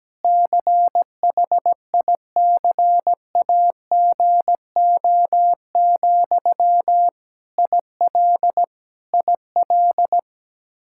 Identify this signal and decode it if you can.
Morse code, heard as a keyed tone, about 17 wpm, 700 Hz. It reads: CHICAGO, IL IL
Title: Practice QSO - Exchange 1 - Beginning - 17wpm